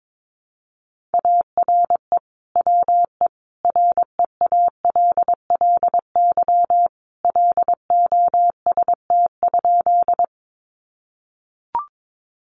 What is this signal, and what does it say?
Morse code, heard as a keyed tone, about 22 wpm, 700 Hz. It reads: ARE WE REALLY LOST?
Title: Sentences from Top 1400 Words 22wpm